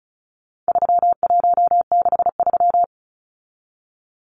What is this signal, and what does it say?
Morse code, heard as a keyed tone, about 34 wpm, 700 Hz. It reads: 3163
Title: Numbers - 4-Digit 35wpm